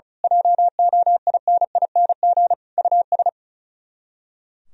Morse code, heard as a keyed tone, about 34 wpm, 700 Hz. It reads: JOINING US
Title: Speed-Racing - Sets of 5 Words 35wpm